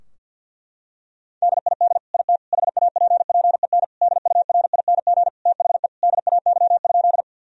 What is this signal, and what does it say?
Morse code, heard as a keyed tone, about 52 wpm, 700 Hz. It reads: DID IT HAPPEN DURING THE DAY?
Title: Sentences from Top 1000 Words - Review Part 2 - 50wpm